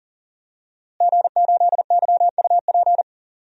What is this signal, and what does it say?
Morse code, heard as a keyed tone, about 40 wpm, 700 Hz. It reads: G8YUP
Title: DXCC-100 Call Signs - 5 Characters - Repeated 2x - 40wpm